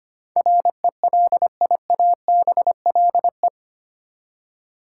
Morse code, heard as a keyed tone, about 25 wpm, 700 Hz. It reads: RELIABLE
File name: Eight Letter Words - 25wpm